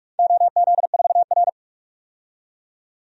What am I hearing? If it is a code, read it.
Morse code, heard as a keyed tone, about 44 wpm, 690 Hz. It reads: OZ4R
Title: DXCC-100 Call Signs - 4 Characters - 45wpm